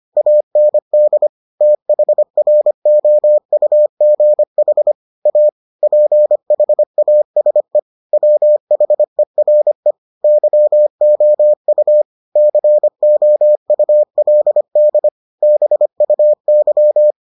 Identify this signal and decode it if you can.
Morse code, heard as a keyed tone, about 25 wpm, 600 Hz. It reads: AND THROUGH A PHASE WHERE YOU COULD BUY